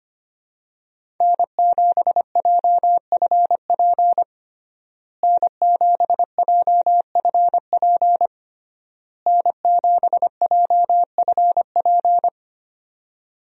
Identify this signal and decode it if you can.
Morse code, heard as a keyed tone, about 25 wpm, 700 Hz. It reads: N7JFP N7JFP N7JFP
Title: USA Call Signs - 5 Characters - Repeated 3x - 25wpm